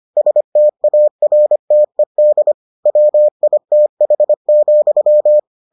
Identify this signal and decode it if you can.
Morse code, heard as a keyed tone, about 25 wpm, 600 Hz. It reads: STARTED WITH,